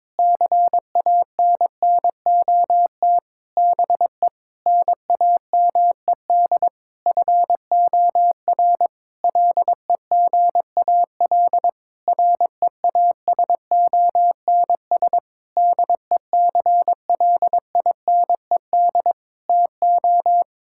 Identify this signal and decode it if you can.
Morse code, heard as a keyed tone, about 22 wpm, 700 Hz. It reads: CANNOT BE NAMED FOR LEGAL REASONS DECLINED TO
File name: practice / 22wpm / Sentences with 8-Word Phrases - 22wpm